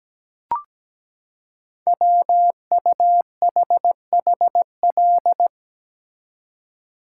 Morse code, heard as a keyed tone, about 17 wpm, 700 Hz. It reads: WUHHL
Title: Code Groups - 5 Letters - 17wpm